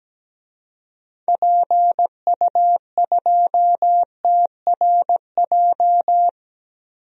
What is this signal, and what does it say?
Morse code, heard as a keyed tone, about 17 wpm, 700 Hz. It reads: PU2TRJ
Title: DXCC-100 Call Signs - 6 Characters - Repeated 3x - 17wpm